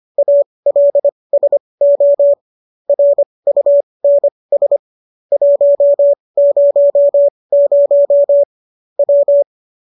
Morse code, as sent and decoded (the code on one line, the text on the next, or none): .- .-.. ... --- / .-. ..- -. ... / .---- ----- ----- / .--
ALSO RUNS 100 W